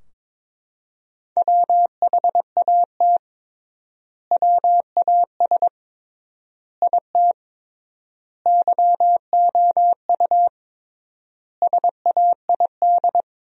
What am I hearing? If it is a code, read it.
Morse code, heard as a keyed tone, about 22 wpm, 700 Hz. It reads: WHAT WAS IT YOU SAID
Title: 3x Word-Spacing - Sentences from Top 200 Words - 22wpm